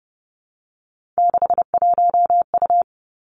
Morse code, heard as a keyed tone, about 30 wpm, 700 Hz. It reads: 61U
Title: Three Challenging Letter-Number Combinations 30wpm